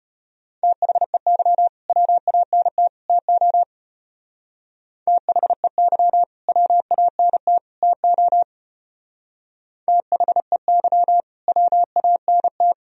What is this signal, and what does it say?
Morse code, heard as a keyed tone, about 34 wpm, 700 Hz. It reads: THEY WANT TO THEY WANT TO THEY WANT